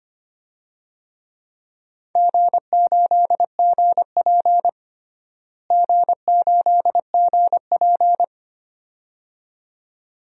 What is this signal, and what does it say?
Morse code, heard as a keyed tone, about 25 wpm, 700 Hz. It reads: G8GP G8GP
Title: DXCC-100 Call Signs - 4 Characters - Repeated 2x - 25wpm